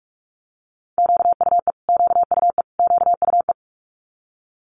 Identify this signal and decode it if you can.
Morse code, heard as a keyed tone, about 46 wpm, 700 Hz. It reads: QUE QUE QUE
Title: Multi-letter Phonemes Repeated 3x 45wpm